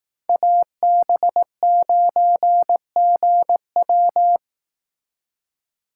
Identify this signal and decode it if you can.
Morse code, heard as a keyed tone, about 18 wpm, 700 Hz. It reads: AB9GW
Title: Speed Racing - USA 2x2 Call Signs 15wpm